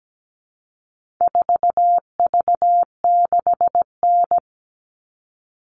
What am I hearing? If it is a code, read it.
Morse code, heard as a keyed tone, about 17 wpm, 700 Hz. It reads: 4V6N
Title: All Cut Numbers - 4-Digit 17wpm